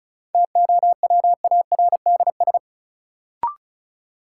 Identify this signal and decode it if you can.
Morse code, heard as a keyed tone, about 36 wpm, 700 Hz. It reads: TOWARDS
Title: Sentences from Top 1100 Words - New Word Drill Alt - 35wpm